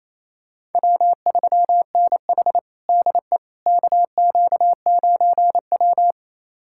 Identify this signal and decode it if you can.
Morse code, heard as a keyed tone, about 28 wpm, 700 Hz. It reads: W3NH DE KQ9W
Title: USA Callsign DE Callsign - 4 Characters - 28wpm